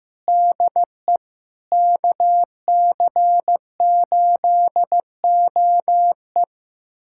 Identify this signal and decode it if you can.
Morse code, heard as a keyed tone, about 15 wpm, 700 Hz. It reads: DE KC8OE
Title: USA Callsign DE Callsign - 5 Characters - 15wpm